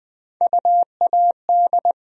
Morse code, heard as a keyed tone, about 20 wpm, 700 Hz. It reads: UAD
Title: USA Call Sign Postfixes - 3 Letters 20wpm